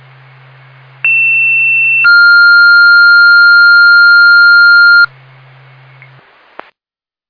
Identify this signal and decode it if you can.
Two-tone page, heard as a one-second tone, then a three-second tone, about 2700 then 1400 Hz. It is heard from 1.0 to 5.1 s.